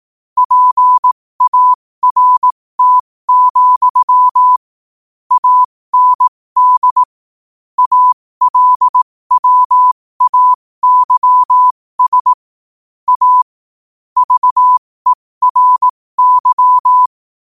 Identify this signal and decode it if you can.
Morse code, heard as a keyed tone, about 18 wpm, spaced as for 15 wpm, 1000 Hz. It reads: PART, AND ALWAYS A VERY
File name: qotdfiles / 15wpm